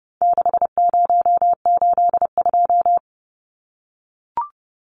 Morse code, heard as a keyed tone, about 30 wpm, 700 Hz. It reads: 6082
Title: Numbers - 4-Digit 30wpm